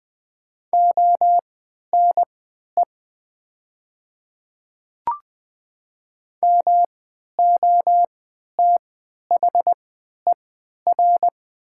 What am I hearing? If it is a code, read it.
Morse code, heard as a keyed tone, about 20 wpm, spaced as for 11 wpm, 700 Hz. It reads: ONE MOTHER
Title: 3x Character Spacing - Top 200 Words - 20wpm